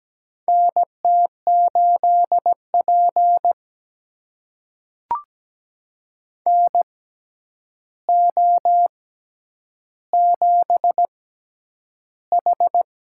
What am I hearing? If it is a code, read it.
Morse code, heard as a keyed tone, about 17 wpm, spaced as for 9 wpm, 700 Hz. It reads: NT8P NO7H
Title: USA Call Signs - 4 Characters - 6x Character Spacing - 17wpm